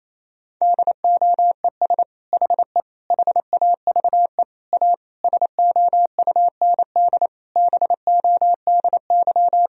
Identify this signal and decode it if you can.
Morse code, heard as a keyed tone, about 27 wpm, 700 Hz. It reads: DOES HE HAVE A SOUND BODY
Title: Sentences from Top 1000 Words - Review Part 2 - 28wpm